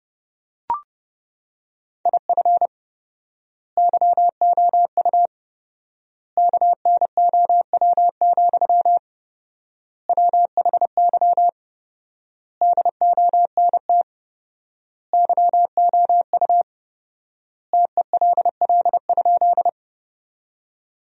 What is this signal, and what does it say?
Morse code, heard as a keyed tone, about 30 wpm, 700 Hz. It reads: IF YOU KNOW, WHY DONT YOU TELL?
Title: 4x Word-Spacing - Sentences from Top 300 Words - 30wpm